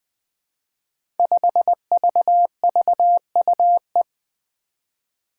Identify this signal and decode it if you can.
Morse code, heard as a keyed tone, about 20 wpm, 700 Hz. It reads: 5VVUE
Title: Five Challenging Letter-Number Combinations 20wpm